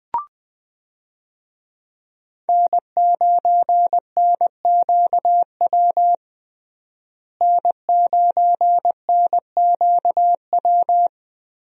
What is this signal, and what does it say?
Morse code, heard as a keyed tone, about 20 wpm, 700 Hz. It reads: N9NQW N9NQW
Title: USA Call Signs - 5 Characters - Repeated 3x - 20wpm